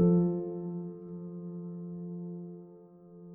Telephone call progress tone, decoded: dial tone